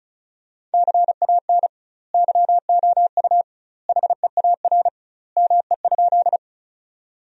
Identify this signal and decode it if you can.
Morse code, heard as a keyed tone, about 36 wpm, 700 Hz. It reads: CAN YOU HEAR ME?
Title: 2x Word-Spacing - Sentences from Top 300 Words - 35wpm